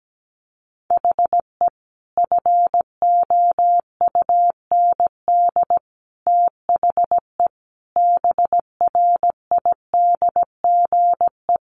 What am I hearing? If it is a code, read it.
Morse code, heard as a keyed tone, about 17 wpm, 700 Hz. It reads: HE FOUND THE BRIDGE
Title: Sentences from Top 1200 Words 17wpm